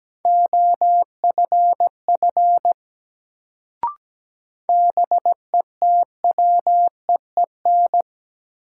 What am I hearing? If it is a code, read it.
Morse code, heard as a keyed tone, about 17 wpm, 700 Hz. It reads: OFF BETWEEN
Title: Sets of 2 Words - New Word Drill - 17wpm